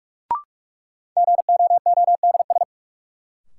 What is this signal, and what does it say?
Morse code, heard as a keyed tone, about 46 wpm, 700 Hz. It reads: GOODS
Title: Sentences from Top 1000 Words - Review Part 6 - New Word Drill - 45wpm